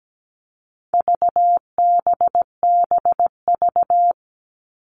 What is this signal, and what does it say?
Morse code, heard as a keyed tone, about 17 wpm, 700 Hz. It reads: VBBV
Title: All Cut Numbers - 4-Digit 17wpm